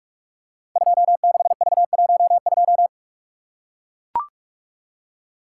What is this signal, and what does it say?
Morse code, heard as a keyed tone, about 44 wpm, 700 Hz. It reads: 26V12
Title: Code Groups - 5 Letter:Numbers - 45wpm